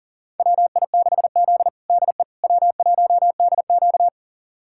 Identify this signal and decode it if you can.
Morse code, heard as a keyed tone, about 40 wpm, 700 Hz. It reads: WI6Z DE W1DQ
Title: USA Callsign DE Callsign - 4 Characters - 40wpm